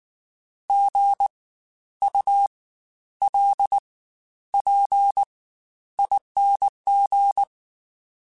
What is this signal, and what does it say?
Morse code, heard as a keyed tone, about 19 wpm, spaced as for 11 wpm, 800 Hz. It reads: GULPING
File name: Spaced Alphas / Spaced Alphas mp3s / Gs sp @18